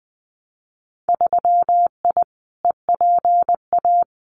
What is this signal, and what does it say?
Morse code, heard as a keyed tone, about 20 wpm, 700 Hz. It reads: 3I EPA
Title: Winter Field Day Exchanges 20wpm